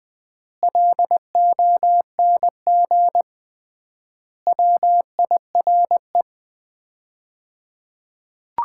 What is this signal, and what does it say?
Morse code, heard as a keyed tone, about 20 wpm, 700 Hz. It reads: LONG WIRE
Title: Common QSO Phrases - 3x Word-Spacing - 20wpm